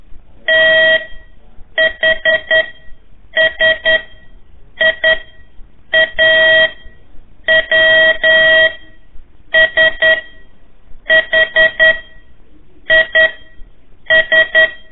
Morse code, heard as a keyed tone, about 11 wpm, spaced as for 7 wpm, 1900 Hz. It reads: THSIA WSHIS
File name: 3 lezione CW 1 AIR